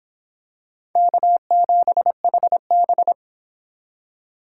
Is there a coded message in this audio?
Yes: Morse code from 1.0 to 3.1 s, heard as a keyed tone, about 24 wpm, 700 Hz. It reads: K7HB